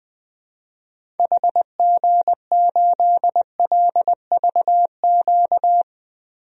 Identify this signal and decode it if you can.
Morse code, heard as a keyed tone, about 21 wpm, 700 Hz. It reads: HG8LVQ
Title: DXCC-100 Call Signs - 6 Characters - Repeated 3x - 20wpm